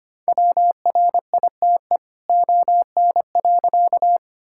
Morse code, heard as a keyed tone, about 25 wpm, 700 Hz. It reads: WRITE ON.
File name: practice / 25wpm / All-Age Appropriate Jokes - 25wpm